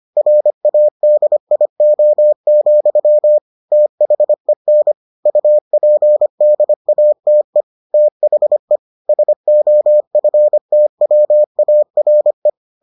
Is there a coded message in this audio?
Yes: Morse code from 0.2 to 12.5 s, heard as a keyed tone, about 25 wpm, 600 Hz. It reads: RADIO, THEN UPDATE THE SOFTWARE